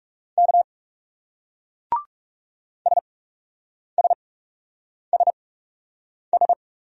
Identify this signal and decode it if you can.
Morse code, heard as a keyed tone, about 37 wpm, spaced as for 8 wpm, 700 Hz. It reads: K SSSS
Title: Speed-Racing Single Letter-Number 30wpm